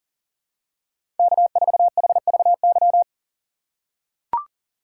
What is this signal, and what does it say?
Morse code, heard as a keyed tone, about 40 wpm, 700 Hz. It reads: K4HVY
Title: USA Call Signs - 5 Characters - 6x Character Spacing - 40wpm